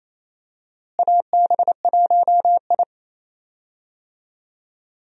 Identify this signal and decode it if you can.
Morse code, heard as a keyed tone, about 28 wpm, 700 Hz. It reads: AB1I
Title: USA Call Signs - 4 Characters - Repeated 3x - 28wpm